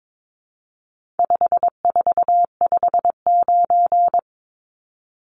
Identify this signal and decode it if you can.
Morse code, heard as a keyed tone, about 22 wpm, 700 Hz. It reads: 5459
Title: Numbers - 4-Digit 22wpm